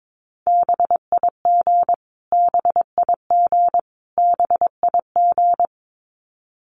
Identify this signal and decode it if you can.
Morse code, heard as a keyed tone, about 22 wpm, 700 Hz. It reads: BIG BIG BIG
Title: Top 500 Words Repeated 22wpm